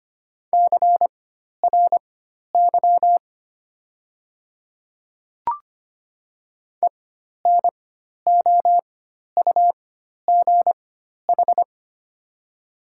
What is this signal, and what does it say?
Morse code, heard as a keyed tone, about 25 wpm, spaced as for 12 wpm, 700 Hz. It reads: CRY ENOUGH
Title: 4x Character Spacing - Top 500 Words - 25wpm